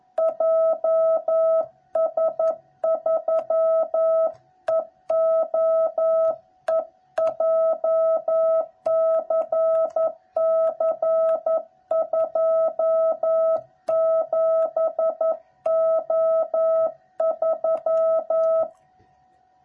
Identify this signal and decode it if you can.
Morse code, heard as a keyed tone, about 11 wpm, 660 Hz. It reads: JS3EOE JCC27O3
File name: jh8cbh / 2021cw / cw06b